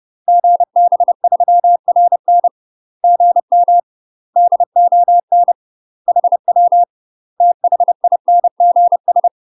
Morse code, heard as a keyed tone, about 30 wpm, 700 Hz. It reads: GB3RN GM DON HW THINGS
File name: UK MASTER QSOs TRAINING / 30WPM / UK MASTER QSOs 0018